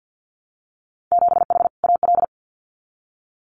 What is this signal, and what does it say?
Morse code, heard as a keyed tone, about 50 wpm, 700 Hz. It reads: 7H AL
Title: Winter Field Day Exchanges Encore 50wpm